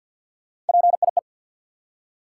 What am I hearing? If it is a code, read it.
Morse code, heard as a keyed tone, about 52 wpm, 700 Hz. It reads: PIE
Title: Three Letter Words - 50wpm